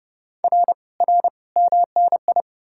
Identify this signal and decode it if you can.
Morse code, heard as a keyed tone, about 40 wpm, 700 Hz. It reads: R R MNI